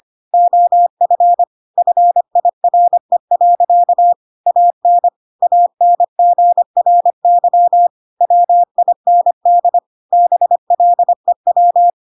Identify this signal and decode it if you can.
Morse code, heard as a keyed tone, about 25 wpm, 700 Hz. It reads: OF FIRE. AN ANGRY WIND BLEW
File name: Pinocchio / Pinocchio ch6 25 25 ik1hge